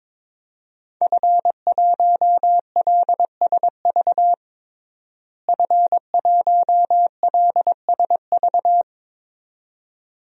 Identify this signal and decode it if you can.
Morse code, heard as a keyed tone, about 22 wpm, 700 Hz. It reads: F1LSV F1LSV
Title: DXCC-100 Call Signs - 5 Characters - Repeated 2x - 22wpm